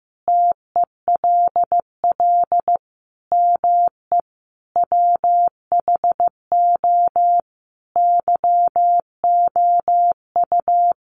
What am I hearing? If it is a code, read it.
Morse code, heard as a keyed tone, about 15 wpm, 700 Hz. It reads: TELL ME WHO YOU
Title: Sentences from Top 700 Words 15wpm